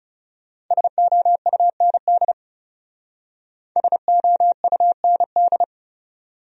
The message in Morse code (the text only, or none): SOUND SOUND